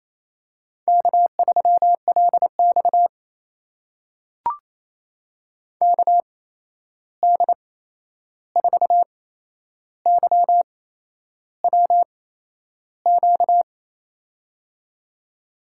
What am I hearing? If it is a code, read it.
Morse code, heard as a keyed tone, about 28 wpm, spaced as for 11 wpm, 700 Hz. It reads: K3LX KD4YWQ
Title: USA Call Signs - 8x Character Spacing - 28wpm